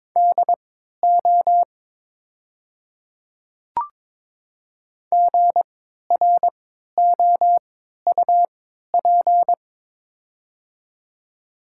Morse code, heard as a keyed tone, about 22 wpm, spaced as for 12 wpm, 700 Hz. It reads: DO GROUP
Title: 3x Character Spacing - Top 500 Words - 22wpm